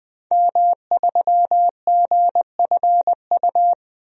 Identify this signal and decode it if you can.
Morse code, heard as a keyed tone, about 21 wpm, 700 Hz. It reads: M3GFU
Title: DXCC-100 Call Signs - 5 Characters - Repeated 2x - 20wpm